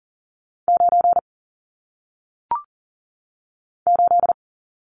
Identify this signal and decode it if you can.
Morse code, heard as a keyed tone, about 39 wpm, 700 Hz. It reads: 9 8
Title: Numbers - 1-Digit 40wpm